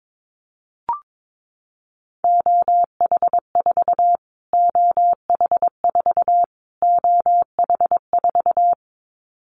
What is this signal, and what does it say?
Morse code, heard as a keyed tone, about 22 wpm, 700 Hz. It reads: OH4 OH4 OH4
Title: Callsign Prefixes for Top 25 DXCC - Repeated 3x - 22wpm